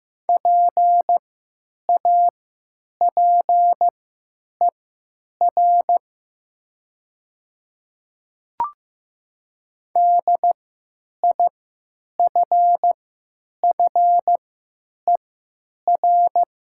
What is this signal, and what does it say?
Morse code, heard as a keyed tone, about 15 wpm, spaced as for 9 wpm, 700 Hz. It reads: PAPER DIFFER
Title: 3x Character Spacing - Top 300 Words - 15wpm